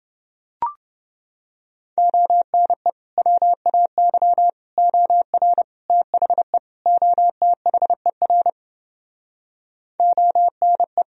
Morse code, heard as a keyed tone, about 30 wpm, 700 Hz. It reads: ONE WAY OR THE OTHER ONE